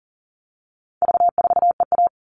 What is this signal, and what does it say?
Morse code, heard as a keyed tone, about 40 wpm, 700 Hz. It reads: V4EA
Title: All Cut Numbers - 4-Digit 40wpm